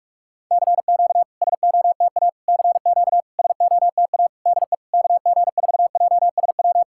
Morse code, heard as a keyed tone, about 44 wpm, 700 Hz. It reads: CQ SOTA CQ SOTA DE KG4JSW
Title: SOTA QSO Practice 45wpm